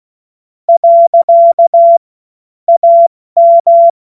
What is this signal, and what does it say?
Morse code, heard as a keyed tone, about 16 wpm, 680 Hz. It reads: . AM
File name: Morsekurs / Witze / Witz 28. September 2021, 70 F80